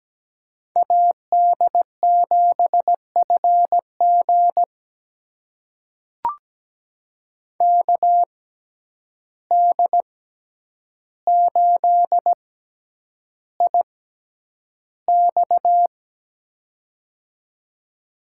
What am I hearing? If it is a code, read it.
Morse code, heard as a keyed tone, about 17 wpm, spaced as for 9 wpm, 700 Hz. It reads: AD7FG KD8IX
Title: USA Call Signs - 5 Characters - 6x Character Spacing - 17wpm